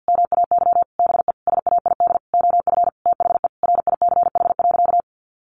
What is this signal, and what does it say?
Morse code, heard as a keyed tone, about 50 wpm, 700 Hz. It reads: MAY BE SAID OF THE RICH.
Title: Sentences from Top 1100 Words 50wpm